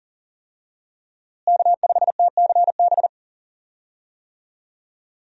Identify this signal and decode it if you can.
Morse code, heard as a keyed tone, about 40 wpm, 700 Hz. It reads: K5TCB
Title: USA Call Signs - 5 Characters - Repeated 2x - 40wpm